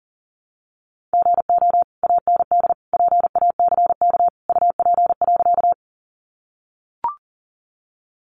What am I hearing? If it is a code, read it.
Morse code, heard as a keyed tone, about 37 wpm, 700 Hz. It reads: GO AND PACK UP.
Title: Sentences from Top 1300 Words 40wpm